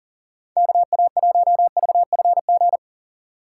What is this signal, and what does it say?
Morse code, heard as a keyed tone, about 40 wpm, 700 Hz. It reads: KA1VFG
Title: USA Call Signs - 6 Characters - 3x Character Spacing - 40wpm